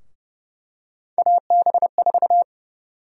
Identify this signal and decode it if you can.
Morse code, heard as a keyed tone, about 30 wpm, 700 Hz. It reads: AB4
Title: USA Call Sign Prefixes - 30wpm